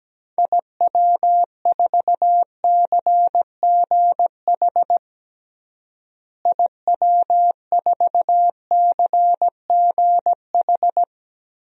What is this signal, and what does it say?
Morse code, heard as a keyed tone, about 17 wpm, 700 Hz. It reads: IW4CGH IW4CGH
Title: DXCC-100 Call Signs - 6 Characters - Repeated 3x - 17wpm